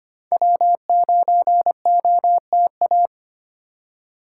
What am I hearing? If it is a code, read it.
Morse code, heard as a keyed tone, about 25 wpm, 700 Hz. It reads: W9OTA
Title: USA Call Signs - 5 Characters - 3x Character Spacing - 25wpm